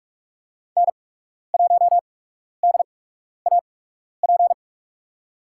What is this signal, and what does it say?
Morse code, heard as a keyed tone, about 41 wpm, spaced as for 12 wpm, 700 Hz. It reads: N1DAP
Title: USA Call Signs - 8x Character Spacing - 45wpm